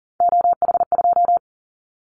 Morse code, heard as a keyed tone, about 42 wpm, 700 Hz. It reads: OH2
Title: Callsign Prefixes for Top 50 DXCC - 40wpm